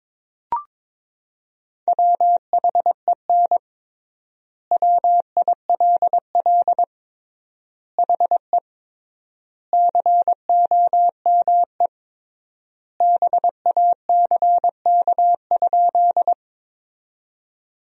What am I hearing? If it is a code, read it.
Morse code, heard as a keyed tone, about 22 wpm, 700 Hz. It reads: WHEN WILL HE COME BACK?
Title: 3x Word-Spacing - Sentences from Top 200 Words - 22wpm